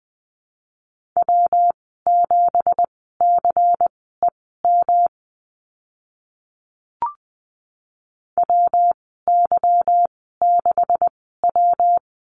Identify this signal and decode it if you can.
Morse code, heard as a keyed tone, about 20 wpm, spaced as for 14 wpm, 700 Hz. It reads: W7CEM WY6W
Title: 167 Lesson 27g - Number 7 - Call Signs 20wpm